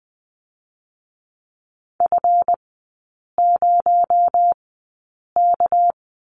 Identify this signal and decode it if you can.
Morse code, heard as a keyed tone, about 20 wpm, spaced as for 8 wpm, 700 Hz. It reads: F0K
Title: 218 Lesson 35e - Number 0 - Three Characters 20wpm